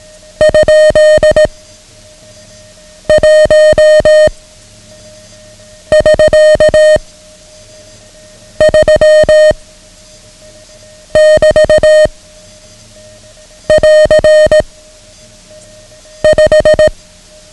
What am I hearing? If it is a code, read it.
Morse code, heard as a keyed tone, about 18 wpm, 620 Hz. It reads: ? 1 <SK> 3 <BT> <AR> 5
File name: ham radio / CodeQuick / Code Quick 09